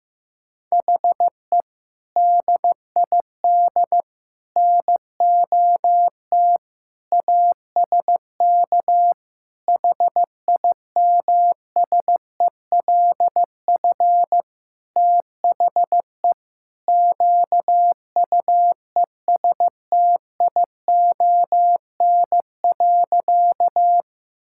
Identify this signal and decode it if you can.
Morse code, heard as a keyed tone, about 15 wpm, 700 Hz. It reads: HE DID NOT ASK HIMSELF THE QUESTION.
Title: Sentences from Top 1600 Words 15wpm